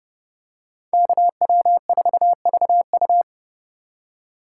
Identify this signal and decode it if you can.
Morse code, heard as a keyed tone, about 28 wpm, 700 Hz. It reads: KW4VU